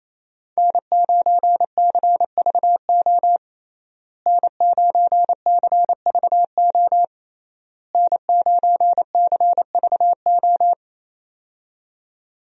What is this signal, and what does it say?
Morse code, heard as a keyed tone, about 28 wpm, 700 Hz. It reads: N9CVO N9CVO N9CVO
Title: USA Call Signs - 5 Characters - Repeated 3x - 28wpm